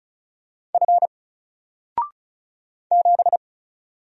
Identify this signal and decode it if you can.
Morse code, heard as a keyed tone, about 33 wpm, 700 Hz. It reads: F 7
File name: Single-Letter-Number-35wpm